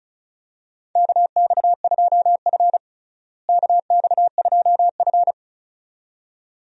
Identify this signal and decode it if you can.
Morse code, heard as a keyed tone, about 36 wpm, 700 Hz. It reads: KX2F KX2F
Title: USA Call Signs - 4 Characters - Repeated 2x - 35wpm